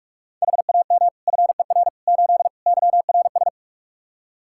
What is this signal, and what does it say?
Morse code, heard as a keyed tone, about 42 wpm, 700 Hz. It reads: HAM FER 8 YRS
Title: Practice QSO - Exchange 3 - More Info - 45wpm